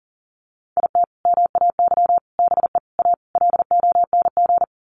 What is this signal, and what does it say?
Morse code, heard as a keyed tone, about 40 wpm, 700 Hz. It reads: IT MAY BE A LONG
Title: Sentences from Top 1100 Words 40wpm